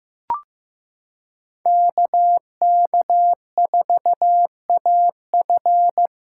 Morse code, heard as a keyed tone, about 15 wpm, 700 Hz. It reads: KK4AF
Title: USA Callsign DE Callsign - 5 Characters - 15wpm